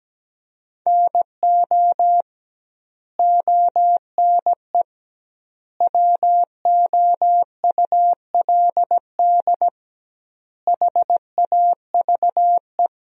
Morse code, heard as a keyed tone, about 17 wpm, 700 Hz. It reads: NO ONE WOULD HAVE
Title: 2x Word-Spacing - Sentences from Top 300 Words - 17wpm